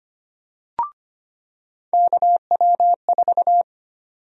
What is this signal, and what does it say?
Morse code, heard as a keyed tone, about 25 wpm, 700 Hz. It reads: KW4
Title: USA Call Sign Prefixes - 25wpm